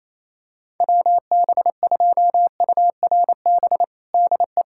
Morse code, heard as a keyed tone, about 28 wpm, 700 Hz. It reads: WB2URB DE
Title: USA Callsign DE Callsign 28wpm